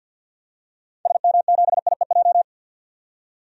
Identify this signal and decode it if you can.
Morse code, heard as a keyed tone, about 52 wpm, 700 Hz. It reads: SM7IEJ
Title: DXCC-100 Call Signs - 6 Characters - 50wpm